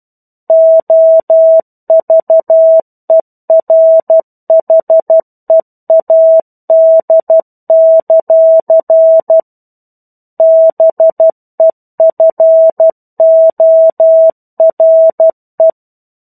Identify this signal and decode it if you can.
Morse code, heard as a keyed tone, about 12 wpm, 650 Hz. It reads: OVERHEAD; BEFORE
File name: 01-Chapter-01 12 WPM